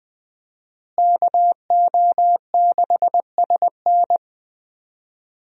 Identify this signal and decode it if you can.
Morse code, heard as a keyed tone, about 21 wpm, 700 Hz. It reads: KO6SN
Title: USA Call Signs - 5 Characters - Repeated 3x - 20wpm